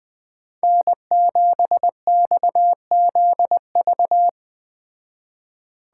Morse code, heard as a keyed tone, about 21 wpm, 700 Hz. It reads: N7XZV